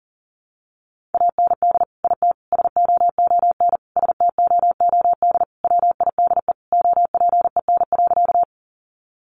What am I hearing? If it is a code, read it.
Morse code, heard as a keyed tone, about 40 wpm, 700 Hz. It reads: AND IT SOON STOOD WIDE OPEN.
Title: Sentences from Top 900 Words 40wpm